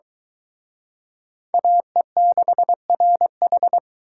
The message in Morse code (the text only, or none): AE6RH